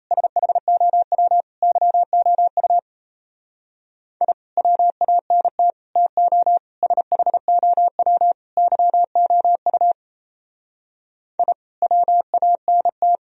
Morse code, heard as a keyed tone, about 34 wpm, 700 Hz. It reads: SHOW YOU I WANT TO SHOW YOU I WANT
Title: Speed-Racing - Sets of 5 Words 28wpm